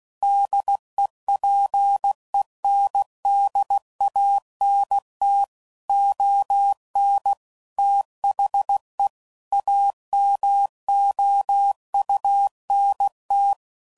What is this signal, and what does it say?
Morse code, heard as a keyed tone, about 16 wpm, 800 Hz. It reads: DEPENDANT ON THE AMOUNT
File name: CW Code Groups @15WPM / Technical Subjects / XC